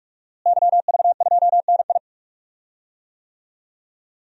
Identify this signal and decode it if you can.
Morse code, heard as a keyed tone, about 45 wpm, 700 Hz. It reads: YV2NI
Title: DXCC-100 Call Signs - 5 Characters - Repeated 2x - 45wpm